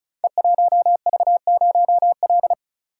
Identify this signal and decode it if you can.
Morse code, heard as a keyed tone, about 36 wpm, 700 Hz. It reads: E1V0L
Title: Code Groups - 5 Letter:Numbers - 35wpm